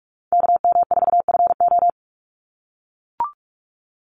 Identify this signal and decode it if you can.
Morse code, heard as a keyed tone, about 44 wpm, 700 Hz. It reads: KM4FO
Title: Call Signs - 45wpm